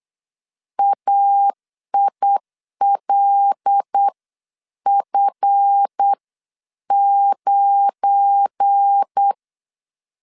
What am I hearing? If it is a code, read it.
Morse code, heard as a keyed tone, about 8 wpm, spaced as for 7 wpm, 800 Hz. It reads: AILF9
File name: LEZIONI / 7 GF9Z / 7 GF9 8wpm 3545